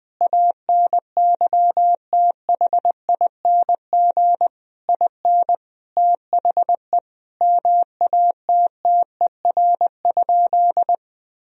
Morse code, heard as a keyed tone, about 21 wpm, 700 Hz. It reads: ANYTHING IN THE MATTER?
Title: Sentences from Top 1000 Words - Review Part 4 - 20wpm